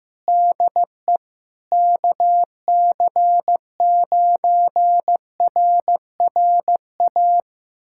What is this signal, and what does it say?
Morse code, heard as a keyed tone, about 15 wpm, 700 Hz. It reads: DE KC9RRA
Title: USA Callsign DE Callsign 15wpm